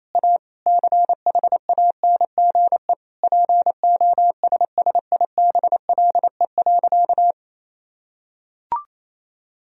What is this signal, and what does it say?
Morse code, heard as a keyed tone, about 28 wpm, 700 Hz. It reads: A CHANGE POSSIBLE.
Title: Sentences from Top 1000 Words - Review Part 1 - 28wpm